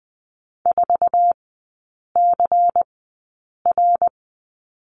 Morse code, heard as a keyed tone, about 20 wpm, spaced as for 8 wpm, 700 Hz. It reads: 4CR
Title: 093 Lesson 16e - Letter c - Three Characters 20wpm